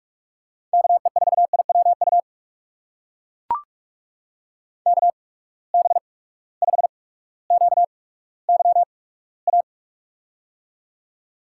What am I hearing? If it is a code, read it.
Morse code, heard as a keyed tone, about 44 wpm, spaced as for 19 wpm, 700 Hz. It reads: KE4IWU KB5QYA
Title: USA Call Signs - 6 Characters - 8x Character Spacing - 45wpm